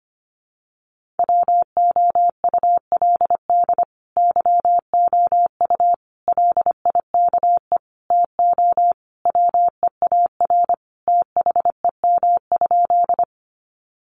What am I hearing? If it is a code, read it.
Morse code, heard as a keyed tone, about 25 wpm, 700 Hz. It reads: WOULD YOU LIKE TO WEAR THEM?
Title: Sentences from Top 1500 Words 25wpm